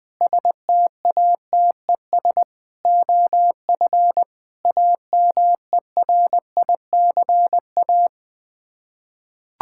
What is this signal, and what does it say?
Morse code, heard as a keyed tone, about 20 wpm, 700 Hz. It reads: STATES OF AMERICA
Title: Sets of 8 Words - 20wpm